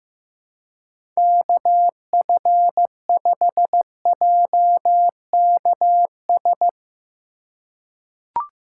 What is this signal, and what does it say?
Morse code, heard as a keyed tone, about 15 wpm, 700 Hz. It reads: KF5JKS
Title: USA Call Signs - 6 Characters - 2x Character Spacing - 15wpm